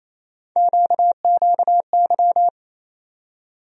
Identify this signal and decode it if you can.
Morse code, heard as a keyed tone, about 28 wpm, 700 Hz. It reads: QQY
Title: USA Call Sign Postfixes - 3 Letters 28wpm